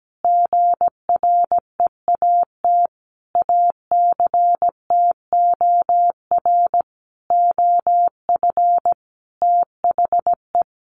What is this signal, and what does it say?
Morse code, heard as a keyed tone, about 17 wpm, 700 Hz. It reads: GREAT ACTOR OF THE
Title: Sentences from Top 1500 Words 17wpm